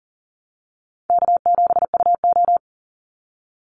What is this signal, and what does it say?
Morse code, heard as a keyed tone, about 40 wpm, 700 Hz. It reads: K7UO
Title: USA Call Signs 40wpm